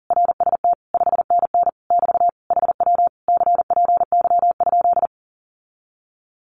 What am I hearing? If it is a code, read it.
Morse code, heard as a keyed tone, about 40 wpm, 700 Hz. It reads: RST 5NN <BT> HW CPY?
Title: Common QSO Phrases Encore 40wpm